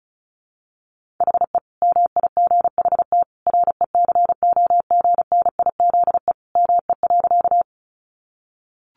Morse code, heard as a keyed tone, about 34 wpm, 700 Hz. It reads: HE MIGHT RECOGNIZE ME.
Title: Sentences from Top 1400 Words 35wpm